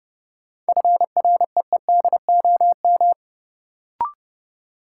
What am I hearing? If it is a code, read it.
Morse code, heard as a keyed tone, about 30 wpm, 700 Hz. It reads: FREEDOM
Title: Sentences from Top 800 Words - New Word Drill - 30wpm